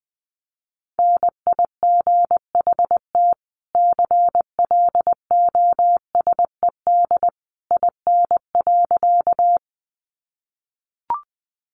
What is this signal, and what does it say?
Morse code, heard as a keyed tone, about 20 wpm, 700 Hz. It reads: NIGHT CLOSED IN.
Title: Sentences from Top 1400 Words 20wpm